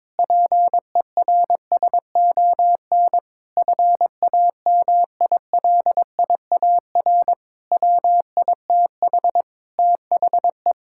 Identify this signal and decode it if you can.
Morse code, heard as a keyed tone, about 21 wpm, 700 Hz. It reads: PERSON FAMILIAR WITH THE
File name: Sentences with 7-Word Phrases - 22wpm